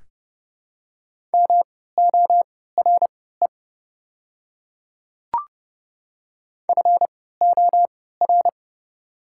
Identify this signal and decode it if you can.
Morse code, heard as a keyed tone, about 30 wpm, spaced as for 17 wpm, 700 Hz. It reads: MORE FOR
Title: 3x Character Spacing - Top 100 Words - 30wpm